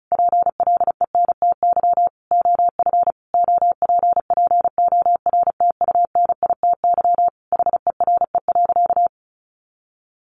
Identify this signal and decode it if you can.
Morse code, heard as a keyed tone, about 34 wpm, 700 Hz. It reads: PLENTY OF OPPORTUNITY HERE.
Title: Sentences from Top 1300 Words 35wpm